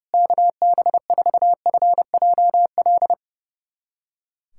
Morse code, heard as a keyed tone, about 30 wpm, 700 Hz. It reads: KB4FJL
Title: USA Callsign DE Callsign - 6 Characters - 30wpm